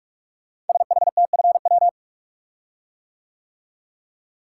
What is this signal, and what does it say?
Morse code, heard as a keyed tone, about 44 wpm, 700 Hz. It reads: SHTFW
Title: Code Groups - 5 Letters - 45wpm